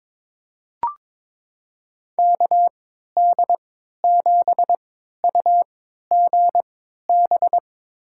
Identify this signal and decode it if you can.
Morse code, heard as a keyed tone, about 22 wpm, spaced as for 13 wpm, 700 Hz. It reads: KD7UGB
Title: USA Call Signs - 6 Characters - 3x Character Spacing - 22wpm